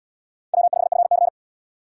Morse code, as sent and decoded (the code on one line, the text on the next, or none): .-- ..... ..-. -.-.
W5FC